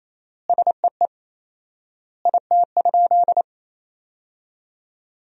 Morse code, heard as a keyed tone, about 27 wpm, 700 Hz. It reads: SEE IT?
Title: 4x Word-Spacing - Sentences from Top 100 Words - 28wpm